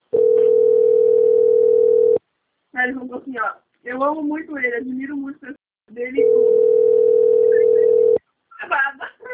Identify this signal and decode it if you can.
Ringback tone (North America), 440 + 480 Hz, with speech.
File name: out- - - - - .21580